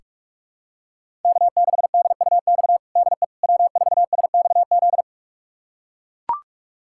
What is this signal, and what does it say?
Morse code, heard as a keyed tone, about 43 wpm, 700 Hz. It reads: K6DUX DE W4SXZ
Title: USA Callsign DE Callsign - 5 Characters - 45wpm